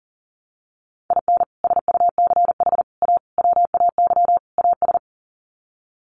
Morse code, heard as a keyed tone, about 40 wpm, 700 Hz. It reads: IN SUCH A WAY AS
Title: Sets of 5 Words Encore 40wpm